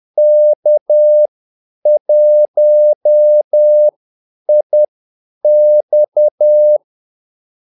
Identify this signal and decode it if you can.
Morse code, heard as a keyed tone, about 10 wpm, spaced as for 8 wpm, 600 Hz. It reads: K1IX
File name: Callsigns-10-8wpm